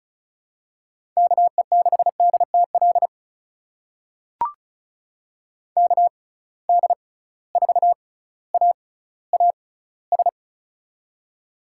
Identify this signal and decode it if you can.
Morse code, heard as a keyed tone, about 36 wpm, spaced as for 18 wpm, 690 Hz. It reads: KE6DTL KD4AAS
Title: USA Call Signs - 6 Characters - 6x Character Spacing - 35wpm